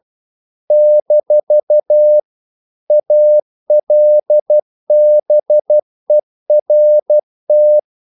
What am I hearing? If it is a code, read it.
Morse code, heard as a keyed tone, about 12 wpm, 600 Hz. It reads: - ALBERT
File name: CW / Practice / EnglishQuotations0112